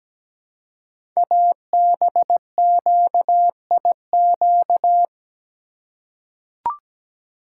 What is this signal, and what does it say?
Morse code, heard as a keyed tone, about 17 wpm, 700 Hz. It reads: ABQIQ
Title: Code Groups - 5 Letters - 17wpm